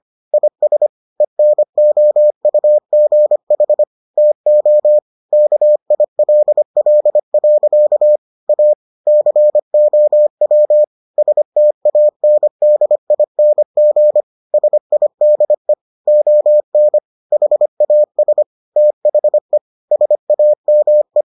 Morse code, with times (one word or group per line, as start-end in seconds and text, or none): 0.3-0.9 IS
1.2-3.8 ENOUGH
4.2-5.0 TO
5.3-8.2 KILL.
8.5-8.7 A
9.1-10.8 COW
11.2-14.2 STANDING
14.5-15.7 SIDE
16.1-17.0 ON
17.3-18.4 HAS
18.8-19.6 THE
19.9-21.2 SAME